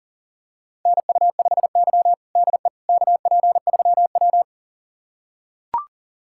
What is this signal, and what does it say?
Morse code, heard as a keyed tone, about 40 wpm, 700 Hz. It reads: NU5Y DE KP3W
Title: USA Callsign DE Callsign - 4 Characters - 40wpm